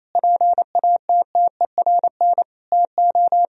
Morse code, heard as a keyed tone, about 28 wpm, 700 Hz. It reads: PATTERN TO
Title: Sentences from Top 1000 Words - Review Part 2 - 28wpm